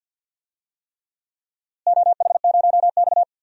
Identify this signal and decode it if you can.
Morse code, heard as a keyed tone, about 47 wpm, 700 Hz. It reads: OH0X
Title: DXCC-100 Call Signs - 4 Characters - Repeated 3x - 50wpm